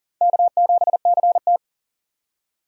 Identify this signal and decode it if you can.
Morse code, heard as a keyed tone, about 50 wpm, 700 Hz. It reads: K7CT
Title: Speed Racing - USA 1x2 Call Signs 40wpm